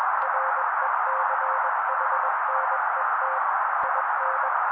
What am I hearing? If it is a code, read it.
Morse code, heard as a keyed tone, about 21 wpm, 530 Hz. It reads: RECHNET IN
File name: daily / 20wpm noise